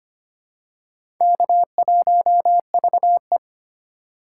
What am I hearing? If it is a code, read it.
Morse code, heard as a keyed tone, about 25 wpm, 700 Hz. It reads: K1VE
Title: USA Call Signs - 3x Character Spacing - 25wpm